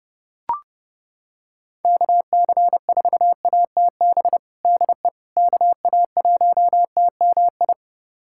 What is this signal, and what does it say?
Morse code, heard as a keyed tone, about 32 wpm, 700 Hz. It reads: KC4ATB DE KA1TMI
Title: USA Callsign DE Callsign - 6 Characters - 30wpm